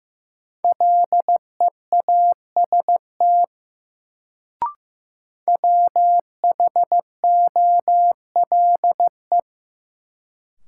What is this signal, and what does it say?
Morse code, heard as a keyed tone, about 15 wpm, 700 Hz. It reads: LEAST WHOLE
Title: Sets of 2 Words - New Word Drill - 15wpm